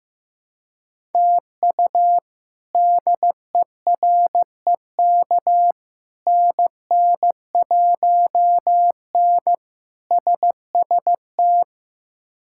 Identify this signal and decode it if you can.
Morse code, heard as a keyed tone, about 15 wpm, 700 Hz. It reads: TU DEREK NN1N SST
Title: K1USN SST QSO Practice - 15wpm